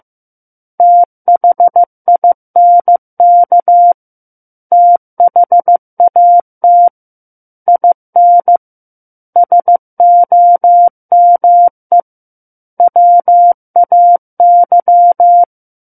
Morse code, heard as a keyed tone, about 15 wpm, 700 Hz. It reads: THINK THAT IN SOME WAY
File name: Book One / 06-Chapter 6 - 15 WPM